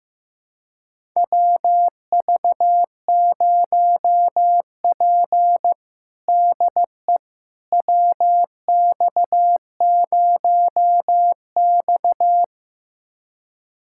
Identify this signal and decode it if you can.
Morse code, heard as a keyed tone, about 15 wpm, 700 Hz. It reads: WV0P DE WX0X
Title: USA Callsign DE Callsign - 4 Characters - 15wpm